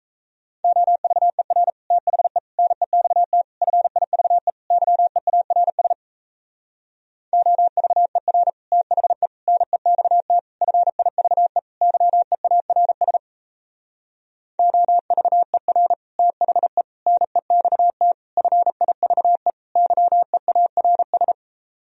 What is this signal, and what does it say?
Morse code, heard as a keyed tone, about 38 wpm, 700 Hz. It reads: OVER THE NEXT FIVE YEARS OVER THE NEXT FIVE YEARS OVER THE NEXT FIVE YEARS